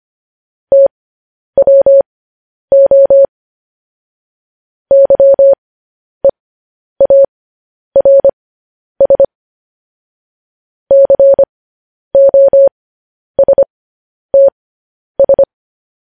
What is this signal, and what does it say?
Morse code, heard as a keyed tone, about 25 wpm, spaced as for 10 wpm, 550 Hz. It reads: TWO YEARS COSTS
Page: news headlines in morse code at 10 wpm